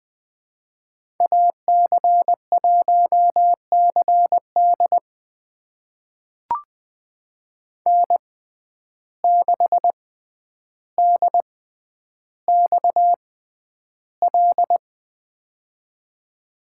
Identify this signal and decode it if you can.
Morse code, heard as a keyed tone, about 20 wpm, spaced as for 10 wpm, 700 Hz. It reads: AC1CD N6DXL
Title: USA Call Signs - 6x Character Spacing - 20wpm